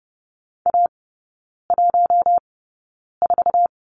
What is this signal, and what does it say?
Morse code, heard as a keyed tone, about 30 wpm, spaced as for 9 wpm, 700 Hz. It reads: A14
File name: Lesson 18e - Letter m - Three Characters 30wpm